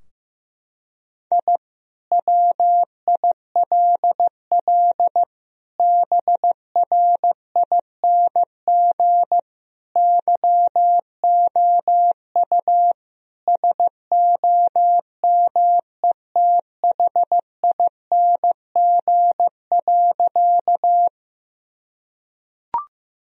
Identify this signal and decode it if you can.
Morse code, heard as a keyed tone, about 15 wpm, 700 Hz. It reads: I WILL BRING YOU SOMETHING.
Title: Sentences from Top 1000 Words - Review Part 3 - 15wpm